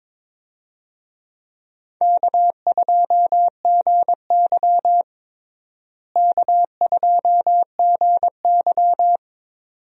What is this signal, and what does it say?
Morse code, heard as a keyed tone, about 21 wpm, 700 Hz. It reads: K2GY K2GY
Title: USA Call Signs - 4 Characters - Repeated 2x - 22wpm